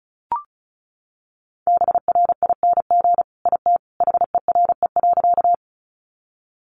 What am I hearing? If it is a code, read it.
Morse code, heard as a keyed tone, about 34 wpm, 700 Hz. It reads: BRING IT HERE.
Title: Sentences from Top 500 Words 35wpm